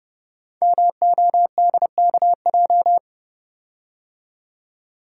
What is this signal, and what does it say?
Morse code, heard as a keyed tone, about 30 wpm, 700 Hz. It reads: MODKJ
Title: Code Groups - 5 Letter:Numbers - 30wpm